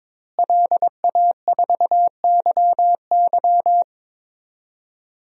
Morse code, heard as a keyed tone, about 22 wpm, 700 Hz. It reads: LA4YY
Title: DXCC-100 Call Signs - 5 Characters - Repeated 2x - 22wpm